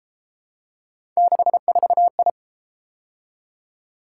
Morse code, heard as a keyed tone, about 31 wpm, 700 Hz. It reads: 64I